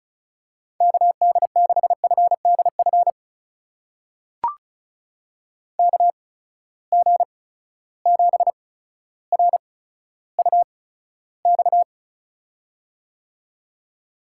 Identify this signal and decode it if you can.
Morse code, heard as a keyed tone, about 36 wpm, spaced as for 15 wpm, 700 Hz. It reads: KD6FDF KG7RUX
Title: USA Call Signs - 6 Characters - 8x Character Spacing - 35wpm